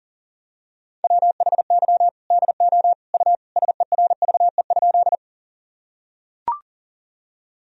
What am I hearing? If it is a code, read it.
Morse code, heard as a keyed tone, about 40 wpm, 700 Hz. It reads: WHY DO U SERVE?
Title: Sentences from Top 400 Words and Abbreviations 40wpm